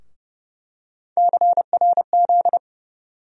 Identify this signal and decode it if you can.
Morse code, heard as a keyed tone, about 30 wpm, 700 Hz. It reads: CRZ
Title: USA Call Sign Postfixes - 3 Letters 30wpm